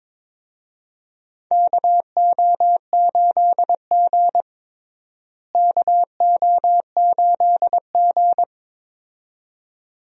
Morse code, heard as a keyed tone, about 22 wpm, 700 Hz. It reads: KO8G KO8G
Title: USA Call Signs - 4 Characters - Repeated 2x - 22wpm